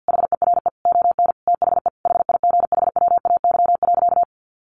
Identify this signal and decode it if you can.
Morse code, heard as a keyed tone, about 50 wpm, 700 Hz. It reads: HERE ON THE HIGHWAY.
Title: Sentences from Top 1400 Words 50wpm